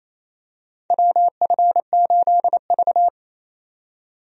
Morse code, heard as a keyed tone, about 28 wpm, 690 Hz. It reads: WF8V
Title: USA Call Signs - 4 Characters - 28wpm